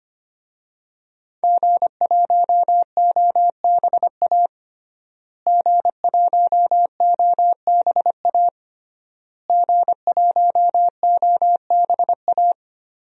Morse code, heard as a keyed tone, about 25 wpm, 700 Hz. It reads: G1OBA G1OBA G1OBA
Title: DXCC-100 Call Signs - 5 Characters - Repeated 3x - 25wpm